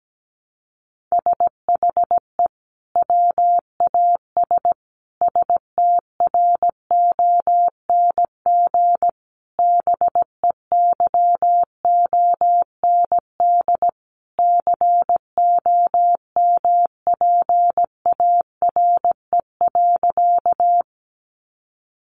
Morse code, heard as a keyed tone, about 17 wpm, 700 Hz. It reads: SHE WAS STRONG BEYOND COMPARE.
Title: Sentences from Top 1300 Words 17wpm